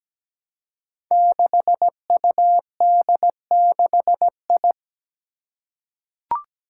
Morse code, heard as a keyed tone, about 17 wpm, 700 Hz. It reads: 6UD6I
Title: Five Challenging Letter-Number Combinations 17wpm